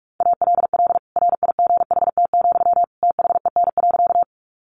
Morse code, heard as a keyed tone, about 44 wpm, 700 Hz. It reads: ALL RIGHT, THEN.
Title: Sentences from Top 200 Words 45wpm